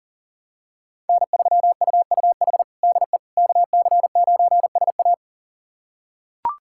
Morse code, heard as a keyed tone, about 40 wpm, 700 Hz. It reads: N3UUH DE KC9SA
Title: USA Callsign DE Callsign - 5 Characters - 40wpm